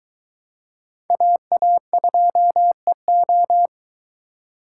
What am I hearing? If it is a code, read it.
Morse code, heard as a keyed tone, about 23 wpm, 700 Hz. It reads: AA2EO